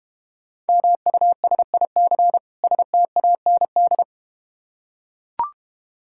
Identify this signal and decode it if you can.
Morse code, heard as a keyed tone, about 30 wpm, 700 Hz. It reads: MUSIC STAND
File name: practice / 25wpm / Difficult Words - Speed-Racing with Hints - 25wpm